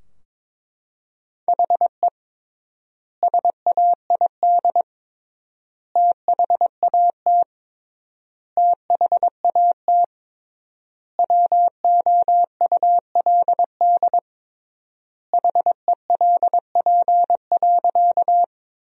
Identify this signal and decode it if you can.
Morse code, heard as a keyed tone, about 22 wpm, 700 Hz. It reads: HE SAID THAT THAT WOULD HELP.
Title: 3x Word-Spacing - Sentences from Top 200 Words - 22wpm